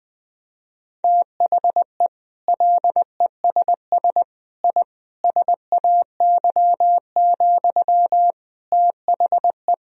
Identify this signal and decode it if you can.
Morse code, heard as a keyed tone, about 20 wpm, 700 Hz. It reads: THE LESS I SAY, THE
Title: Sentences from Top 1000 Words - Review Part 2 - 20wpm